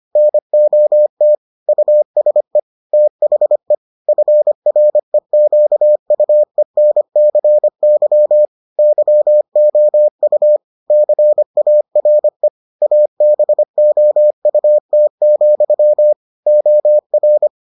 Morse code, heard as a keyed tone, about 25 wpm, 600 Hz. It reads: NOT USE THE FREQUENCY YOU CARE ABOUT, OR